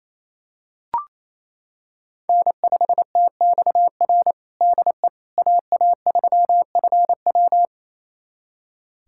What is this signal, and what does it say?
Morse code, heard as a keyed tone, about 27 wpm, 700 Hz. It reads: N5TXR DE AA3FW
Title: USA Callsign DE Callsign - 5 Characters - 28wpm